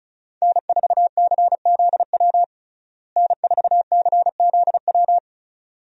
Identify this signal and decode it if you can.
Morse code, heard as a keyed tone, about 36 wpm, 700 Hz. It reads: N4CZW N4CZW
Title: USA Call Signs - 5 Characters - Repeated 2x - 35wpm